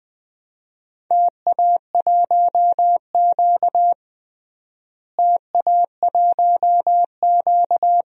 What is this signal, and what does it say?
Morse code, heard as a keyed tone, about 20 wpm, 700 Hz. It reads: TA1Q TA1Q
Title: DXCC-100 Call Signs - 4 Characters - Repeated 2x - 20wpm